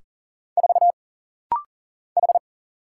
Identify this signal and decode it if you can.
Morse code, heard as a keyed tone, about 40 wpm, 700 Hz. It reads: 4 H
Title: Single Letter-Number Alt - Rapid-Fire 40wpm